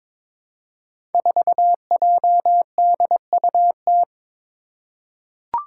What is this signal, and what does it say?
Morse code, heard as a keyed tone, about 22 wpm, 700 Hz. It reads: 4JDUT
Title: Five Challenging Letter-Number Combinations 22wpm